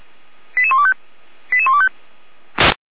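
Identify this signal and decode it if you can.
Five-tone selective call (ZVEI), heard as quick runs of five tones, tones of about 70 ms. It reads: 88125 88125
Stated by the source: ZVEI five-tone selective call 89375, received on an SDR.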